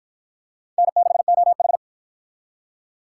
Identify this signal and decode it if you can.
Morse code, heard as a keyed tone, about 46 wpm, 700 Hz. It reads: N6OH